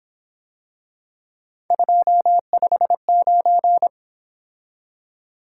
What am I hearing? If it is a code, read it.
Morse code, heard as a keyed tone, about 26 wpm, 700 Hz. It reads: 259
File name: Speed-Racing - Three Letter-Number Combinations - 22wpm